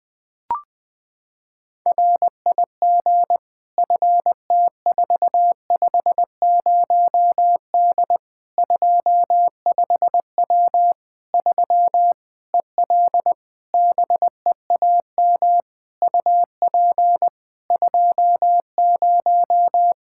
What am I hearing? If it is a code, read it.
Morse code, heard as a keyed tone, about 20 wpm, 700 Hz. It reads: RIG FT450D 25W 3 EL BEAM UP 20
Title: Practice QSO - Exchange 2 - Info - 20wpm